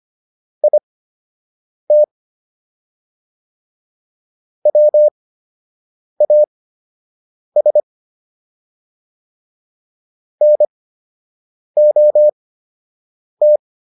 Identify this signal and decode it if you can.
Morse code, heard as a keyed tone, about 25 wpm, spaced as for 7 wpm, 600 Hz. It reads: IT WAS NOT